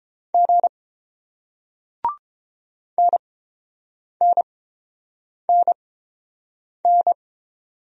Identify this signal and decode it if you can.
Morse code, heard as a keyed tone, about 30 wpm, spaced as for 7 wpm, 690 Hz. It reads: G NNNN